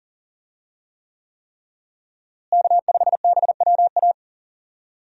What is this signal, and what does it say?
Morse code, heard as a keyed tone, about 37 wpm, 700 Hz. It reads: K5BWA
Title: USA Call Signs - 5 Characters - Repeated 3x - 40wpm